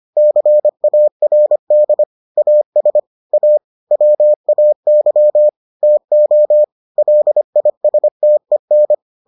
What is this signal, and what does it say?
Morse code, heard as a keyed tone, about 25 wpm, 600 Hz. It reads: CARD AS A WAY TO LISTEN